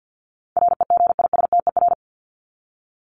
Morse code, heard as a keyed tone, about 50 wpm, 700 Hz. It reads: REGISTER
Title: Top 1001-1250 Words 50wpm